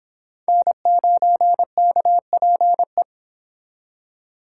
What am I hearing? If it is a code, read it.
Morse code, heard as a keyed tone, about 30 wpm, 700 Hz. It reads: N9KPE